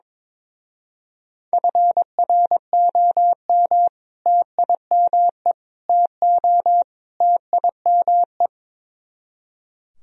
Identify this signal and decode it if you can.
Morse code, heard as a keyed tone, about 24 wpm, 700 Hz. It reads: FROM TIME TO TIME
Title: Speed-Racing - Sets of 4 Words 22wpm